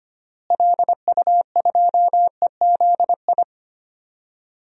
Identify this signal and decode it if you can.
Morse code, heard as a keyed tone, about 25 wpm, 700 Hz. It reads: LU2EZI
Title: DXCC-100 Call Signs - 6 Characters - Repeated 2x - 25wpm